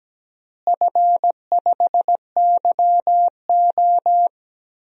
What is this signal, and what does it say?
Morse code, heard as a keyed tone, about 17 wpm, 700 Hz. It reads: F5YO
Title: DXCC-100 Call Signs - 4 Characters - Repeated 2x - 17wpm